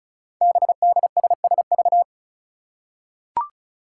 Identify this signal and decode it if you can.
Morse code, heard as a keyed tone, about 36 wpm, 700 Hz. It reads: BDSSV
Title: Five Challenging Letter-Number Combinations 35wpm